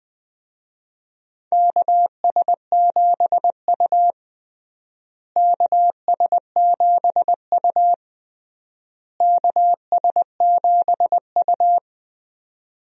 Morse code, heard as a keyed tone, about 21 wpm, 700 Hz. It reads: KS7U KS7U KS7U
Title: USA Call Signs - 4 Characters - Repeated 3x - 20wpm